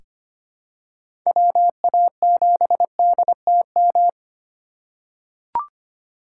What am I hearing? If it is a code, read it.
Morse code, heard as a keyed tone, about 25 wpm, 700 Hz. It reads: WA7DTM
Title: USA Call Signs - 6 Characters - 2x Character Spacing - 25wpm